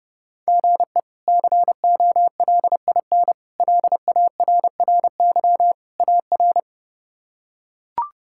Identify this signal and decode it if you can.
Morse code, heard as a keyed tone, about 32 wpm, 700 Hz. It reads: GE COLIN LARRY AR
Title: K1USN SST QSO Practice - 30wpm